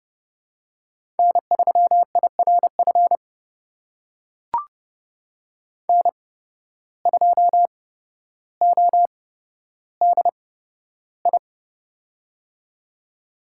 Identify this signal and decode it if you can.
Morse code, heard as a keyed tone, about 30 wpm, spaced as for 13 wpm, 700 Hz. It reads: N3IRF N2ODI
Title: USA Call Signs - 5 Characters - 8x Character Spacing - 30wpm